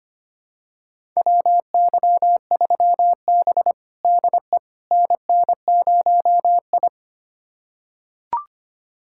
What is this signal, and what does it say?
Morse code, heard as a keyed tone, about 25 wpm, 700 Hz. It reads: WY3B DE NN0I
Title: USA Callsign DE Callsign - 4 Characters - 25wpm